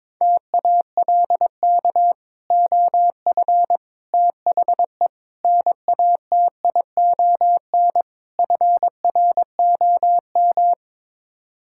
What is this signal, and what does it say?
Morse code, heard as a keyed tone, about 22 wpm, 700 Hz. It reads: TALK OF THE NATION FROM
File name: practice / 22wpm / Build A Phrase - 5-Word Phrases - 22wpm